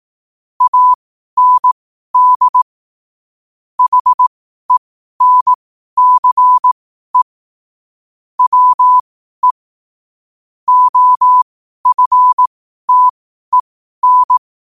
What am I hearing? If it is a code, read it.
Morse code, heard as a keyed tone, about 18 wpm, spaced as for 12 wpm, 1000 Hz. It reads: AND HENCE WE OFTEN
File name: qotdfiles / 12wpm